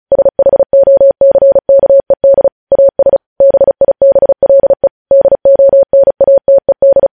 Morse code, heard as a keyed tone, about 36 wpm, 550 Hz. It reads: SHOCKED AS BIBLE DONATED
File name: podcast / News headlines-1x-35x35